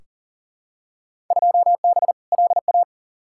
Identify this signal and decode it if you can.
Morse code, heard as a keyed tone, about 39 wpm, 700 Hz. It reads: 2B LA
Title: Field Day Exchanges Encore - 40wpm